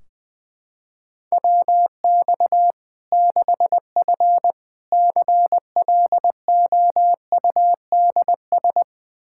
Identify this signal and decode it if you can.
Morse code, heard as a keyed tone, about 21 wpm, 700 Hz. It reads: WX 6F CLOUDS
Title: Practice QSO - Exchange 2 - Info - 20wpm